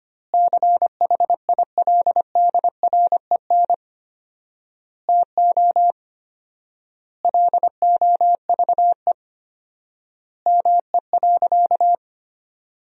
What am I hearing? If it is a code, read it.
Morse code, heard as a keyed tone, about 25 wpm, 700 Hz. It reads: CHILDREN TO LOVE ME.
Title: 4x Word-Spacing - Sentences from Top 400 Words - 25wpm